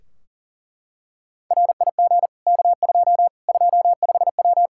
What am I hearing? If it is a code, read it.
Morse code, heard as a keyed tone, about 40 wpm, 700 Hz. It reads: RIG K2 25W
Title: Practice QSO - Exchange 2 - Info - 40wpm